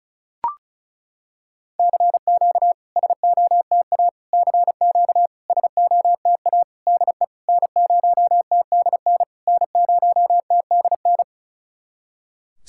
Morse code, heard as a keyed tone, about 36 wpm, 700 Hz. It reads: CQ SOTA CQ SOTA DE N0TDN N0TDN
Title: SOTA QSO Practice 35wpm